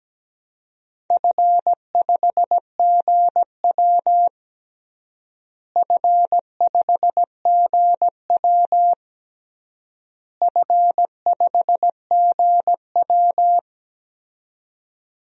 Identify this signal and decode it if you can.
Morse code, heard as a keyed tone, about 17 wpm, 700 Hz. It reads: F5GW F5GW F5GW
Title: DXCC-100 Call Signs - 4 Characters - Repeated 3x - 17wpm